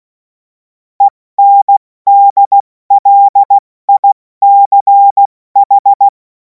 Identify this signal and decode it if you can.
Morse code, heard as a keyed tone, about 16 wpm, 800 Hz. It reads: ENDLICH
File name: Morsekurs / Witze / Witz 10. März 2026 70 F80